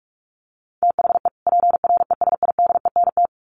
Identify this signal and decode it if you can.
Morse code, heard as a keyed tone, about 46 wpm, 700 Hz. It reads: THE PRESIDENT
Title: Sets of 2 Words 45wpm